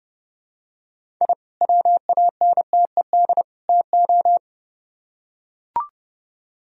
Morse code, heard as a keyed tone, about 28 wpm, 700 Hz. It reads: I WANTED TO